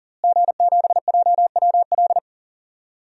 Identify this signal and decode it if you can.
Morse code, heard as a keyed tone, about 38 wpm, 700 Hz. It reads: G7JWL
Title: DXCC-100 Call Signs - 5 Characters - Repeated 3x - 40wpm